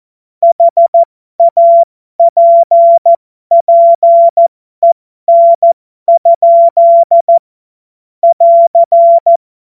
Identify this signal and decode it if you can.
Morse code, heard as a keyed tone, about 14 wpm, spaced as for 12 wpm, 680 Hz. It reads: HAPPEN? <AR>
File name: Morsekurs / Witze